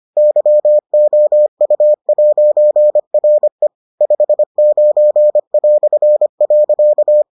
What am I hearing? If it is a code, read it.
Morse code, heard as a keyed tone, about 25 wpm, 600 Hz. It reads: YOU'RE 59".